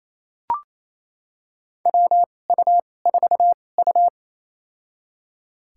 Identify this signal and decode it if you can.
Morse code, heard as a keyed tone, about 28 wpm, spaced as for 20 wpm, 700 Hz. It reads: WU4U
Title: USA Call Signs - 4 Characters - 2x Character Spacing - 28wpm